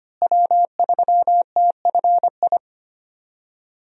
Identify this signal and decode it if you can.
Morse code, heard as a keyed tone, about 25 wpm, 700 Hz. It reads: W3TFI
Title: USA Call Signs - 5 Characters - Repeated 3x - 25wpm